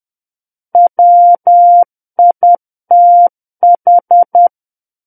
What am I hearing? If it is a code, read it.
Morse code, heard as a keyed tone, about 10 wpm, 700 Hz. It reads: WITH
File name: Book One / 01-Chapter 1 - 10 WPM